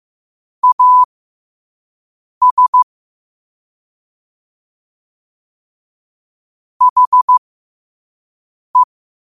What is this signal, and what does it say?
Morse code, heard as a keyed tone, about 15 wpm, spaced as for 5 wpm, 1000 Hz. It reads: AS HE
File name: 2026-02-15-5wpm